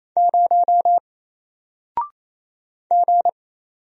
Morse code, heard as a keyed tone, about 27 wpm, 700 Hz. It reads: 0 G
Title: Single Character (Letter:Number:Punctuation:Prosign) 28wpm